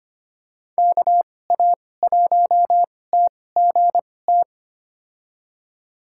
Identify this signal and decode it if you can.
Morse code, heard as a keyed tone, about 25 wpm, spaced as for 18 wpm, 700 Hz. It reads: KA1TGT
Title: USA Call Signs - 6 Characters - 2x Character Spacing - 25wpm